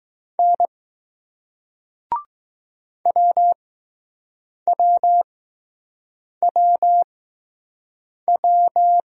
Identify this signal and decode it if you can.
Morse code, heard as a keyed tone, about 20 wpm, spaced as for 6 wpm, 700 Hz. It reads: N WWWW